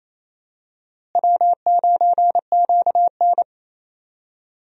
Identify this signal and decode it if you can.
Morse code, heard as a keyed tone, about 28 wpm, 700 Hz. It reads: W9QN
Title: USA Call Signs - 4 Characters - Repeated 2x - 28wpm